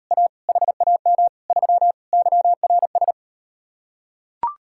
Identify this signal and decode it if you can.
Morse code, heard as a keyed tone, about 39 wpm, 700 Hz. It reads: A HAM 3 YRS